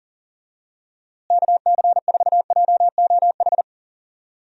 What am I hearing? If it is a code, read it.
Morse code, heard as a keyed tone, about 37 wpm, 700 Hz. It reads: KC4JOH
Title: USA Call Signs - 6 Characters - Repeated 2x - 40wpm